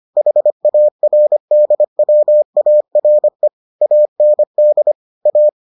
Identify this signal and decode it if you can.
Morse code, heard as a keyed tone, about 25 wpm, 600 Hz. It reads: HARDWARE AND A